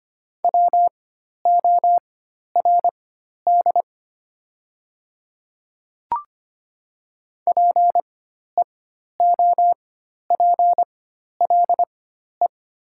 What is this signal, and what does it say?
Morse code, heard as a keyed tone, about 25 wpm, spaced as for 12 wpm, 700 Hz. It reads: WORD PEOPLE
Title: 4x Character Spacing - Top 300 Words - 25wpm